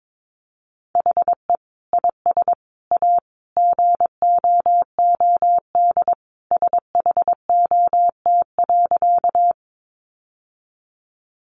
Morse code, heard as a keyed tone, about 22 wpm, 700 Hz. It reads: HE IS A GOOD SHOT.
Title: Sentences from Top 1100 Words 22wpm